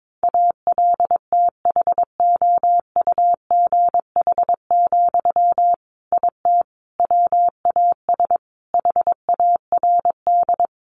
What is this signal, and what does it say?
Morse code, heard as a keyed tone, about 22 wpm, 700 Hz. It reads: ALTHOUGH, IT WAS HARD
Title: Sentences from Top 600 Words 22wpm